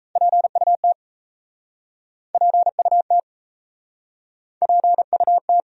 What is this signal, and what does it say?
Morse code, heard as a keyed tone, about 37 wpm, 700 Hz. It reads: PUT PUT PUT